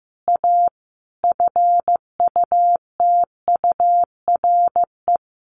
Morse code, heard as a keyed tone, about 15 wpm, 700 Hz. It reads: A FUTURE
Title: Sentences from Top 1500 Words 15wpm